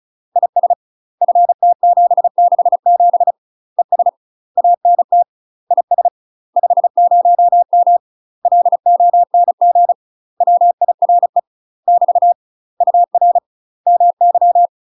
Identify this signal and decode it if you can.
Morse code, heard as a keyed tone, about 34 wpm, 700 Hz. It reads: IS FT767 ES ANT IS 50M LONG WIRE <BT> UR MY
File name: UK MASTER QSOs TRAINING / 35WPM / UK MASTER QSOs 0003